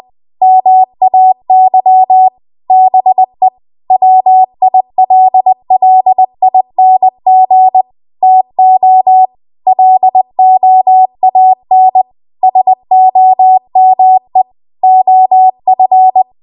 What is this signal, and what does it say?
Morse code, heard as a keyed tone, about 20 wpm, 750 Hz. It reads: MAY BE WILLING TO LOAN SOME OF